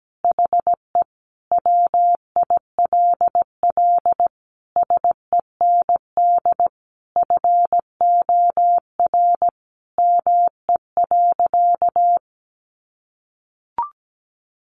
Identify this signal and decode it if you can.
Morse code, heard as a keyed tone, about 17 wpm, 700 Hz. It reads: HE WILL SEND FOR ME.
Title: Sentences from Top 1500 Words 17wpm